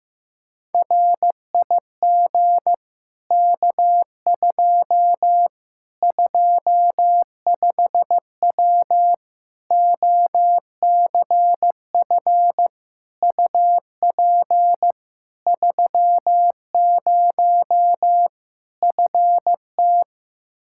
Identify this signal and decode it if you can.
Morse code, heard as a keyed tone, about 15 wpm, 700 Hz. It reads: RIG K2 25W OCF UP 30 FT
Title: Practice QSO - Exchange 2 - Info - 15wpm